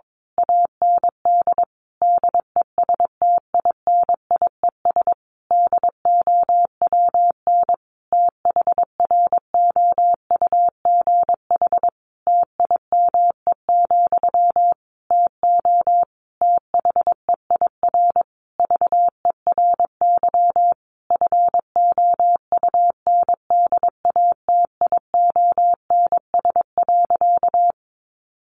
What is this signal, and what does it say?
Morse code, heard as a keyed tone, about 22 wpm, 700 Hz. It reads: AND DESTINIES DOWN THROUGH TIME, TO THEIR VERY FOUNDATIONS.
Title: A Sound Of Thunder 22wpm